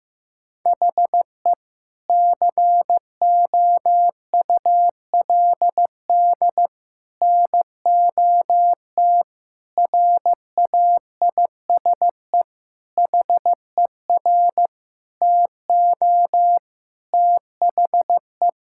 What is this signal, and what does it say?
Morse code, heard as a keyed tone, about 15 wpm, 700 Hz. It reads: HE COULD NOT RAISE HER TO THE
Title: Sentences from Top 1600 Words 15wpm